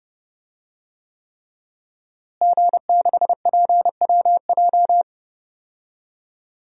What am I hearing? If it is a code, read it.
Morse code, heard as a keyed tone, about 31 wpm, 700 Hz. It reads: G6PWJ
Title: DXCC-100 Call Signs - 5 Characters - Repeated 3x - 30wpm